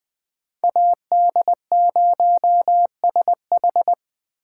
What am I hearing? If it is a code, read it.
Morse code, heard as a keyed tone, about 20 wpm, 700 Hz. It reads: AD0SH
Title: Speed Racing - USA Call Signs 17wpm